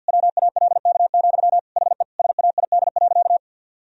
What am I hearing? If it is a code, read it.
Morse code, heard as a keyed tone, about 52 wpm, 700 Hz. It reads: WALK, HE SAID.
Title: Sentences from Top 1000 Words - Review Part 1 - 50wpm